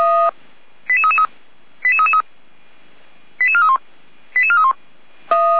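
Five-tone selective call (ZVEI), heard as quick runs of five tones, tones of about 70 ms. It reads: 88333 88333 88431 88431
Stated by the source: ZVEI five-tone selective call 88335 received by SDR, from the Feuerwehr Orlamünde audio server.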